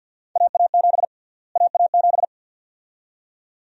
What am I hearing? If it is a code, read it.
Morse code, heard as a keyed tone, about 47 wpm, 700 Hz. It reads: AA7 AA7
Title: USA Call Sign Prefixes - Repeated 2x - 50wpm